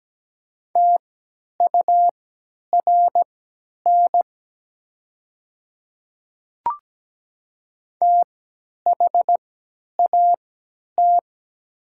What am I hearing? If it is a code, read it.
Morse code, heard as a keyed tone, about 17 wpm, spaced as for 10 wpm, 700 Hz. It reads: TURN THAT
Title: 3x Character Spacing - Top 500 Words - 17wpm